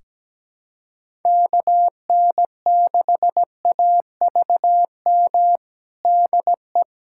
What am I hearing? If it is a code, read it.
Morse code, heard as a keyed tone, about 17 wpm, 700 Hz. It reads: KN6AVM DE
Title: USA Callsign DE Callsign 17wpm